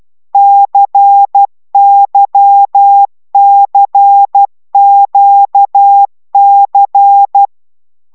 Morse code, heard as a keyed tone, about 12 wpm, 800 Hz. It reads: CYCQC
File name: taviro tanulas / cw 75 15 10